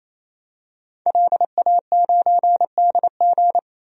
Morse code, heard as a keyed tone, about 28 wpm, 700 Hz. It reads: LA9DG
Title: DXCC-100 Call Signs - 5 Characters - 28wpm